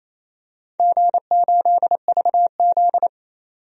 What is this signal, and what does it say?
Morse code, heard as a keyed tone, about 28 wpm, 700 Hz. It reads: G8VZ
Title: DXCC-100 Call Signs - 4 Characters - 28wpm